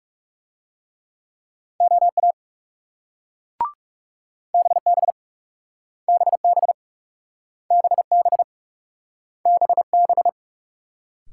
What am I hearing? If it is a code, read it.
Morse code, heard as a keyed tone, about 38 wpm, 700 Hz. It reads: OA BB BB BB BB